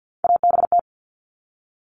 Morse code, heard as a keyed tone, about 50 wpm, 700 Hz. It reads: ABT
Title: Abbreviations 50wpm